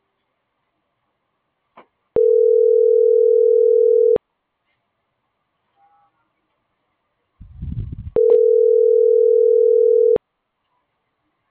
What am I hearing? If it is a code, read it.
Ringback tone (North America), 440 + 480 Hz.
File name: CENTRAL GENERAL / out alexia.jauregui